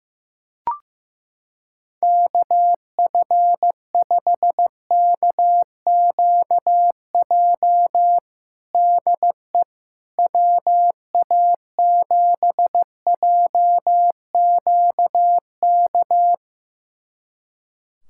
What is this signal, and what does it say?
Morse code, heard as a keyed tone, about 15 wpm, 700 Hz. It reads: KF5KQJ DE WA7JQK
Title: USA Callsign DE Callsign 15wpm